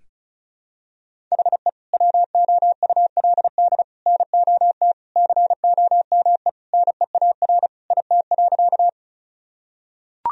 Morse code, heard as a keyed tone, about 36 wpm, 700 Hz. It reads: HE WOULD NOT COME NEAR IT.
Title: Sentences from Top 1000 Words - Review Part 1 - 35wpm